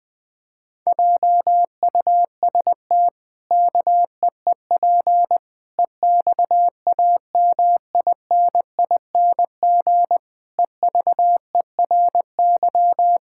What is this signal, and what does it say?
Morse code, heard as a keyed tone, about 20 wpm, 700 Hz. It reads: JUST KEEP EXAMINING EVERY
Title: Pangram Challenge v2 - 20wpm